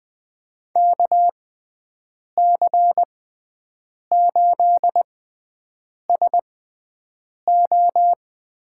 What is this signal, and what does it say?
Morse code, heard as a keyed tone, about 20 wpm, spaced as for 7 wpm, 700 Hz. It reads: KC8SO
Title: USA Call Signs - 6 Characters - 6x Character Spacing - 20wpm